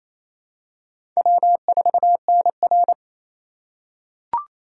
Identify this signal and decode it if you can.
Morse code, heard as a keyed tone, about 27 wpm, 700 Hz. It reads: W4NR
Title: USA Call Signs - 4 Characters - 2x Character Spacing - 28wpm